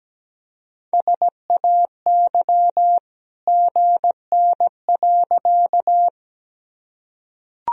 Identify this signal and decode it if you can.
Morse code, heard as a keyed tone, about 17 wpm, 700 Hz. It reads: SAY GN.
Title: Sentences from Top 300 Words and Abbreviations 17wpm